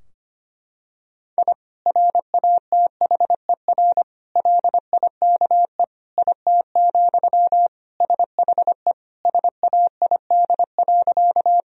Morse code, heard as a keyed tone, about 25 wpm, 700 Hz. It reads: I RATHER LIKE IT, SHE SAID.
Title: Sentences from Top 1000 Words - Review Part 4 - 25wpm